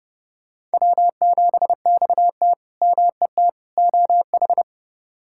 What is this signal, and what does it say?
Morse code, heard as a keyed tone, about 30 wpm, 700 Hz. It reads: W7XT MET OH
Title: K1USN SST QSO Practice - 30wpm